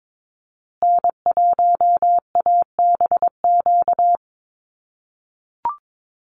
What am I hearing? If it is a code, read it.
Morse code, heard as a keyed tone, about 22 wpm, 700 Hz. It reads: N1ABQ
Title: USA Call Signs Repeated 3x 22wpm